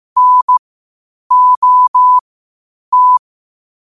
Morse code, heard as a keyed tone, about 15 wpm, spaced as for 8 wpm, 1000 Hz. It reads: NOT
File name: qotdfiles / 8wpm